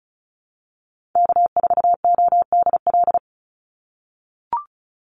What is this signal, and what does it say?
Morse code, heard as a keyed tone, about 34 wpm, 700 Hz. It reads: K4ODL
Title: Call Signs - 35wpm